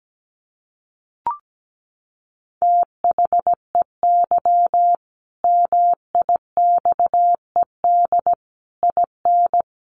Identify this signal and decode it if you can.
Morse code, heard as a keyed tone, about 17 wpm, 700 Hz. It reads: THEY MIXED IN
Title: Sentences from Top 1500 Words 17wpm